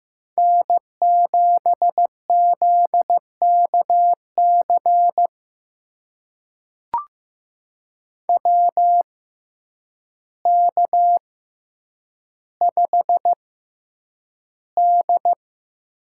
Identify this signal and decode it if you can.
Morse code, heard as a keyed tone, about 15 wpm, spaced as for 8 wpm, 700 Hz. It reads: N7ZKC WK5D
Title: USA Call Signs - 6x Character Spacing - 15wpm